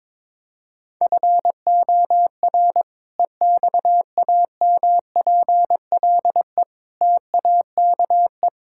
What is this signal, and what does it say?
Morse code, heard as a keyed tone, about 22 wpm, 700 Hz. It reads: FOR EXAMPLE TAKE
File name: practice / 22wpm / Sentences with 2-Word Phrases - 22wpm